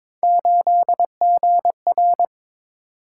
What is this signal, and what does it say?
Morse code, heard as a keyed tone, about 22 wpm, 700 Hz. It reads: 8GR
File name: practice / 22wpm / Three Letter-Number Combinations - 22wpm